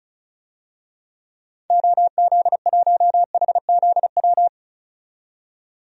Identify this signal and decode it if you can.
Morse code, heard as a keyed tone, about 36 wpm, 700 Hz. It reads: OZ1HZW
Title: DXCC-100 Call Signs - 6 Characters - 35wpm